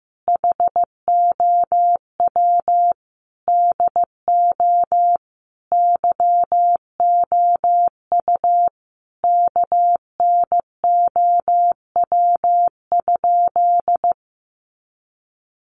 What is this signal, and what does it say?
Morse code, heard as a keyed tone, about 15 wpm, 700 Hz. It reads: HOW DO YOU KNOW?
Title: Sentences from Top 100 Words 15wpm